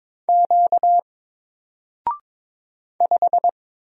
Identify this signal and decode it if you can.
Morse code, heard as a keyed tone, about 22 wpm, 700 Hz. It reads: Q 5
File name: Single-Letter-Number-22wpm